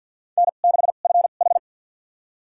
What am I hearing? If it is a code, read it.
Morse code, heard as a keyed tone, about 52 wpm, spaced as for 36 wpm, 700 Hz. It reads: N6FH
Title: USA Call Signs - 4 Characters - 2x Character Spacing - 50wpm